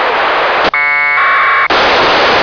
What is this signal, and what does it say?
Packet radio (AX.25, as in APRS).